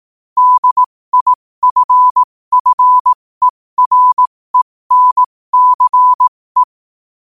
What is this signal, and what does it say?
Morse code, heard as a keyed tone, about 18 wpm, spaced as for 15 wpm, 1000 Hz. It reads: DIFFERENCE
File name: qotdfiles / 15wpm